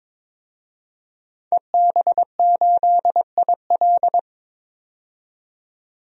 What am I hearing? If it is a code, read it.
Morse code, heard as a keyed tone, about 22 wpm, 700 Hz. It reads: EB8IL
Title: DXCC-100 Call Signs - 5 Characters - 22wpm